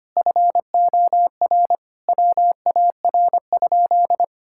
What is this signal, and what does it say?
Morse code, heard as a keyed tone, about 25 wpm, 700 Hz. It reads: FOR WAR?
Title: Sentences from Top 1000 Words - Review Part 5 - 25wpm